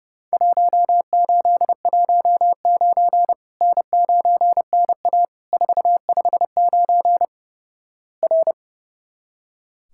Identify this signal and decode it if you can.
Morse code, heard as a keyed tone, about 28 wpm, 700 Hz. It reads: 1819 N9NA 459 R
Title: Worked All Europe DX Contest - QTC Exchange - 30wpm